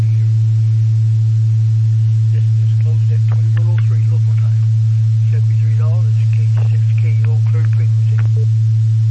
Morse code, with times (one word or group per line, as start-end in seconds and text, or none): none